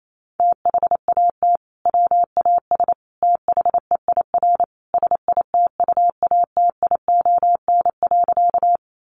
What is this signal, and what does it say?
Morse code, heard as a keyed tone, about 28 wpm, 700 Hz. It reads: THAT WAS THEIR SITUATION.
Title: Sentences from Top 1200 Words 28wpm